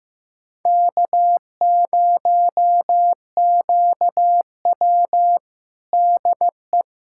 Morse code, heard as a keyed tone, about 15 wpm, 700 Hz. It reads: K0QW DE
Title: USA Callsign DE Callsign - 4 Characters - 15wpm